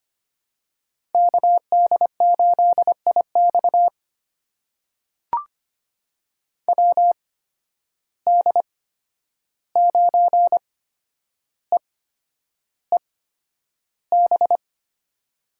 Morse code, heard as a keyed tone, about 25 wpm, spaced as for 10 wpm, 700 Hz. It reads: KD8IX WD9EEB
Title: USA Call Signs - 8x Character Spacing - 25wpm